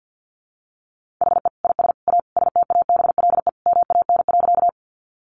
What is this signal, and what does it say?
Morse code, heard as a keyed tone, about 50 wpm, 700 Hz. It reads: HE IS A STABLE MAN.
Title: Sentences from Top 1500 Words 50wpm